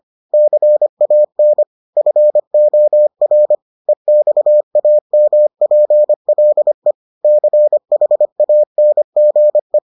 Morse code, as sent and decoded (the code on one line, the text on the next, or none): -.-. .- -. / ..-. --- .-. / . -..- .- -- .--. .-.. / -.-. .... .- -. --. .
CAN FOR EXAMPLE CHANGE